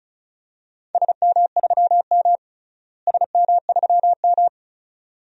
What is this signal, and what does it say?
Morse code, heard as a keyed tone, about 34 wpm, 700 Hz. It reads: SM3M SM3M
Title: DXCC-100 Call Signs - 4 Characters - Repeated 2x - 35wpm